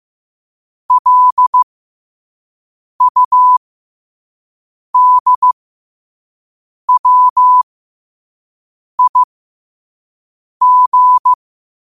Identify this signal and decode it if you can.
Morse code, heard as a keyed tone, about 15 wpm, spaced as for 5 wpm, 1000 Hz. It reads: LUDWIG
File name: qotdfiles / 5wpm